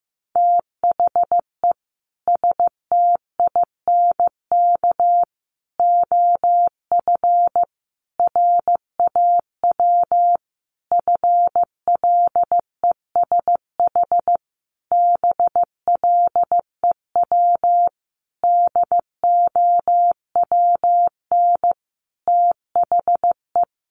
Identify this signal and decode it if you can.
Morse code, heard as a keyed tone, about 15 wpm, 700 Hz. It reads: THE STINK OF RAW FLESH BLEW DOWN THE
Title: A Sound Of Thunder 15wpm